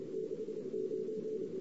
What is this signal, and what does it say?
Morse code, heard as a keyed tone, about 28 wpm, 440 Hz, with noise. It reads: 69
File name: CW / 2008-ss-cw-k5tr-op-n5rz -14:00